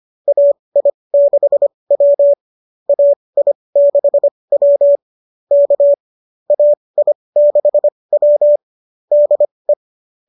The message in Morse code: AI6W AI6W K AI6W DE

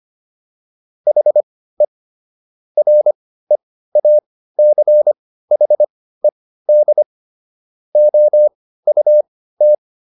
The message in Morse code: HE REACHED OUT

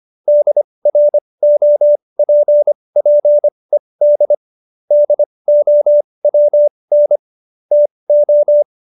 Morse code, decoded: DROPPED DOWN TO

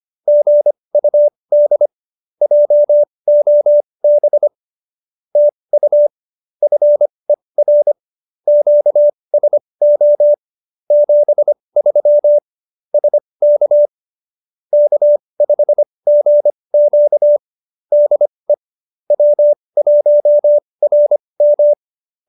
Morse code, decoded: GUD JOB TU FER QSO 73 SK K5GQ DE W1RM

